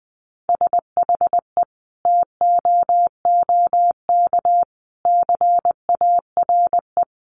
Morse code, heard as a keyed tone, about 20 wpm, 700 Hz. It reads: SHE TOOK CARE
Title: Sentences from Top 1300 Words 20wpm